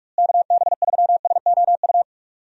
Morse code, heard as a keyed tone, about 41 wpm, 700 Hz. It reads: KB3SOU
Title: USA Call Signs - 6 Characters - Repeated 3x - 45wpm